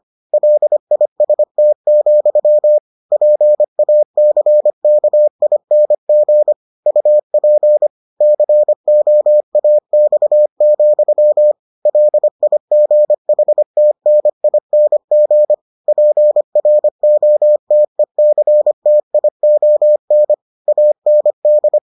Morse code, heard as a keyed tone, about 25 wpm, 600 Hz. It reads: LIST, PACKING UP COAX, LIGHTNING PROTECTION AND